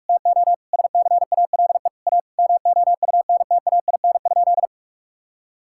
Morse code, heard as a keyed tone, about 46 wpm, 690 Hz. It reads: TO SCALE A MOUNTAIN?
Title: Sentences from Top 1000 Words - Review Part 5 - 45wpm